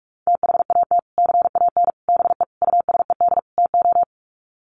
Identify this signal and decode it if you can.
Morse code, heard as a keyed tone, about 44 wpm, 700 Hz. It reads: THAT CAN BE USED TO
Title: Sets of 5 Words Encore 45wpm